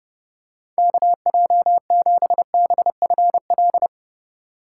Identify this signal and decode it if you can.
Morse code, heard as a keyed tone, about 30 wpm, 700 Hz. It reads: KJ7BFL
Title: USA Call Signs - 6 Characters - 3x Character Spacing - 30wpm